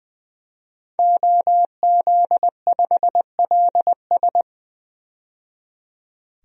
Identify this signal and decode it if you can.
Morse code, heard as a keyed tone, about 20 wpm, 700 Hz. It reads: OZ5LS
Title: DXCC-100 Call Signs - 5 Characters - 20wpm